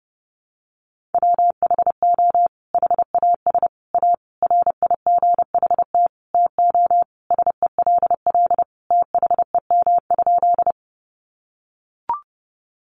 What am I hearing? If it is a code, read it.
Morse code, heard as a keyed tone, about 30 wpm, 700 Hz. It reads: WHO HAS A RIGHT TO SELL THEM?
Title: Sentences from Top 1100 Words 30wpm